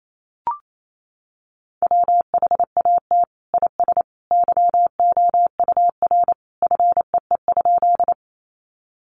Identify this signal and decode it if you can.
Morse code, heard as a keyed tone, about 28 wpm, 700 Hz. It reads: WHAT IS YOUR FEE?
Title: Sentences from Top 1500 Words 28wpm